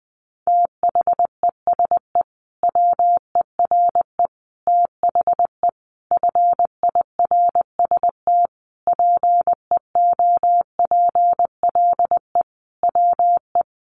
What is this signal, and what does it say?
Morse code, heard as a keyed tone, about 20 wpm, 700 Hz. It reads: THESE WERE THE FIRST PEOPLE WE
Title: Sentences from Top 300 Words 20wpm